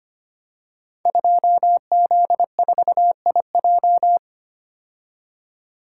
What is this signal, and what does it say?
Morse code, heard as a keyed tone, about 25 wpm, 700 Hz. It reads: 2Z4IJ
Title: Code Groups - 5 Letter:Numbers - 25wpm